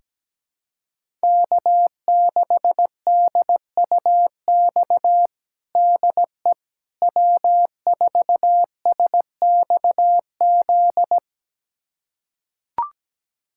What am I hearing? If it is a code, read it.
Morse code, heard as a keyed tone, about 17 wpm, 700 Hz. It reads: K6DUX DE W4SXZ
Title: USA Callsign DE Callsign - 5 Characters - 17wpm